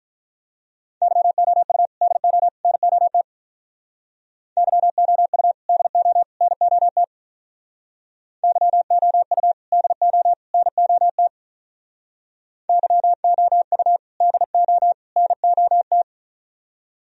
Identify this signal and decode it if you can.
Morse code, heard as a keyed tone, about 44 wpm, 700 Hz. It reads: YOU DO NOT YOU DO NOT YOU DO NOT YOU DO NOT